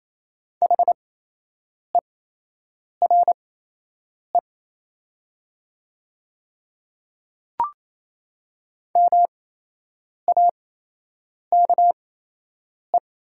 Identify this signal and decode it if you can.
Morse code, heard as a keyed tone, about 28 wpm, spaced as for 8 wpm, 690 Hz. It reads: HERE MAKE
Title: 8x Character Spacing - Top 200 Words - 28wpm